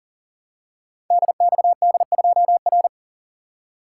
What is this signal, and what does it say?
Morse code, heard as a keyed tone, about 40 wpm, 700 Hz. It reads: DXD2R
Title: Code Groups - 5 Letter:Numbers - 40wpm